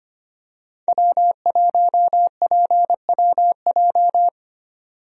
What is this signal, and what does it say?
Morse code, heard as a keyed tone, about 25 wpm, 700 Hz. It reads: W1PWJ
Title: USA Call Signs - 5 Characters - 25wpm